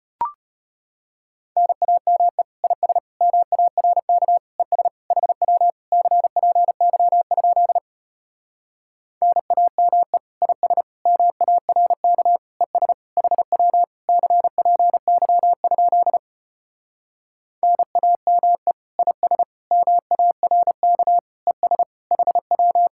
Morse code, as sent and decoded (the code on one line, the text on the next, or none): -. .- -- . / .. ... / -- .- .-. -.- / . ... / .... .-- / -.-. .--. -.-- ..--.. / -. .- -- . / .. ... / -- .- .-. -.- / . ... / .... .-- / -.-. .--. -.-- ..--.. / -. .- -- . / .. ... / -- .- .-. -.- / . ... / .... .--
NAME IS MARK ES HW CPY? NAME IS MARK ES HW CPY? NAME IS MARK ES HW